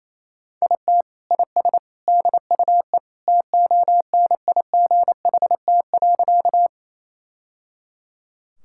Morse code, heard as a keyed tone, about 28 wpm, 700 Hz. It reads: IT IS DUE TONIGHT.
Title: Sentences from Top 1000 Words - Review Part 6 - 28wpm